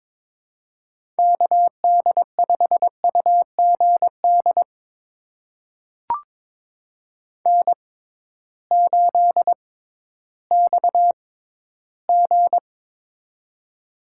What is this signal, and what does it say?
Morse code, heard as a keyed tone, about 23 wpm, spaced as for 13 wpm, 700 Hz. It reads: KD5UGD N8XG
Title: USA Call Signs - 6x Character Spacing - 22wpm